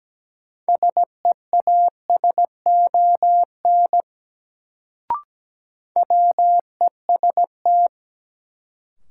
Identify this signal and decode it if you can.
Morse code, heard as a keyed tone, about 17 wpm, 700 Hz. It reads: SEASON WEST
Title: Sets of 5 Words - New Word Drill - 17wpm